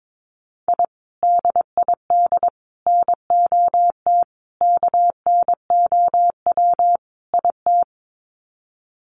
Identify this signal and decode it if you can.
Morse code, heard as a keyed tone, about 22 wpm, 700 Hz. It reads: I DID NOT KNOW IT
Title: Sets of 5 Words Encore2 22wpm